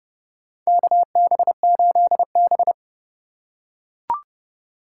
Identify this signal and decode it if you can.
Morse code, heard as a keyed tone, about 31 wpm, 700 Hz. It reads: KB8B
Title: USA Call Signs - 4x Character Spacing - 30wpm